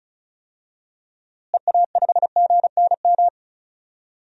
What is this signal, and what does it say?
Morse code, heard as a keyed tone, about 36 wpm, 700 Hz. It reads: EA5GNM
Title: DXCC-100 Call Signs - 6 Characters - Repeated 2x - 35wpm